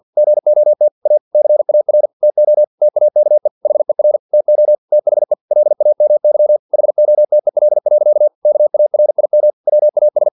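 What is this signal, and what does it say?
Morse code, heard as a keyed tone, about 50 wpm, 600 Hz. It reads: GOT A CAR TO TAKE HER TO THE LAMY HOTEL. KARIM WAS
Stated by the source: Morse practice text from an English-language novel at 49 wpm.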